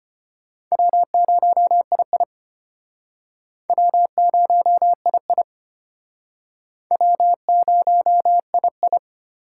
Morse code, heard as a keyed tone, about 33 wpm, 700 Hz. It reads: W0II W0II W0II